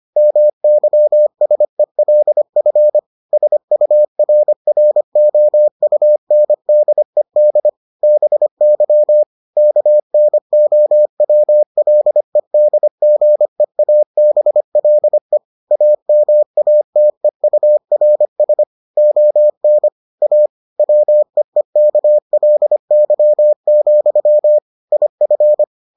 Morse code, heard as a keyed tone, about 25 wpm, 600 Hz. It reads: MYSELF SURROUNDED BY KNOWLEDGEABLE AMATEURS ON A WEEKLY, IF